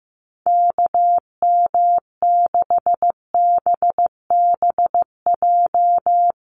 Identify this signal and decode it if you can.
Morse code, heard as a keyed tone, about 14 wpm, 700 Hz. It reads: KM6BBJ
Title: USA Call Signs Repeated 3x 15wpm